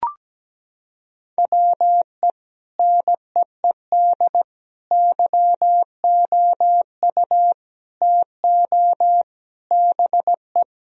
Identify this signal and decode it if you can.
Morse code, heard as a keyed tone, about 17 wpm, 700 Hz. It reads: WE NEED YOU TO BE
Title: Sentences from Top 1000 Words - Review Part 3 - 17wpm